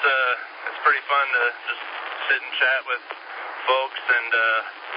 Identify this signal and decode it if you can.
Emergency Alert System (SAME) tones, heard as harsh buzzing bursts, with speech. It is heard from 0.4 to 4.5 s.